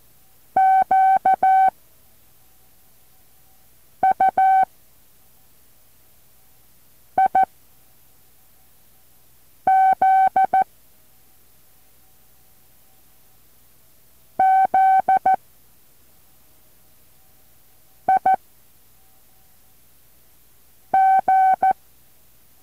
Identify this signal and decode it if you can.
Morse code, heard as a keyed tone, about 14 wpm, 760 Hz. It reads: Q U I Z Z I G
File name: intl morse code2